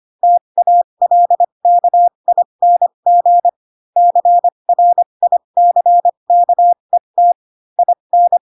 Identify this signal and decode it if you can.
Morse code, heard as a keyed tone, about 25 wpm, spaced as for 22 wpm, 700 Hz. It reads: TALKING CRICKET IN
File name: Pinocchio / Pinocchio ch4 18 25 ik1hge